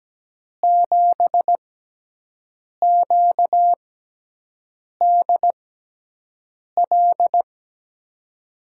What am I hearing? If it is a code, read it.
Morse code, heard as a keyed tone, about 17 wpm, spaced as for 6 wpm, 700 Hz. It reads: 7QDL
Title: USA Call Signs - 6 Characters - 6x Character Spacing - 17wpm